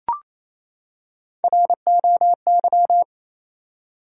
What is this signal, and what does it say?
Morse code, heard as a keyed tone, about 28 wpm, 700 Hz. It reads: ROY
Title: USA Common Names - Max 3 Letters - 28wpm